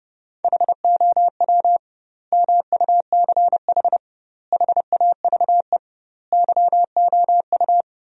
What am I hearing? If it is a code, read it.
Morse code, heard as a keyed tone, about 31 wpm, 700 Hz. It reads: HOW MUCH HAVE YOU
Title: 2x Word-Spacing - Sentences from Top 300 Words - 30wpm